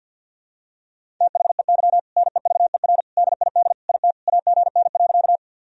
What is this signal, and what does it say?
Morse code, heard as a keyed tone, about 50 wpm, 700 Hz. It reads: THEY NEVER DID IT AGN.
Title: Sentences from Top 300 Words and Abbreviations 50wpm